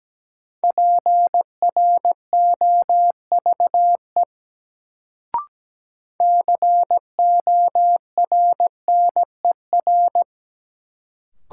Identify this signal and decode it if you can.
Morse code, heard as a keyed tone, about 17 wpm, 700 Hz. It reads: PROVE CORNER
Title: Sentences from Top 1600 Words - New Word Drill Alt - 17wpm